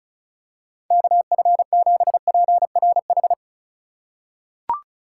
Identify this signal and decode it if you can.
Morse code, heard as a keyed tone, about 34 wpm, 700 Hz. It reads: KF7PRH
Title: USA Call Signs - 4x Character Spacing - 35wpm